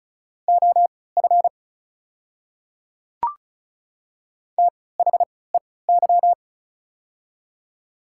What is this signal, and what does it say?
Morse code, heard as a keyed tone, about 36 wpm, spaced as for 20 wpm, 700 Hz. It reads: OF THEY
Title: 3x Character Spacing - Top 200 Words - 35wpm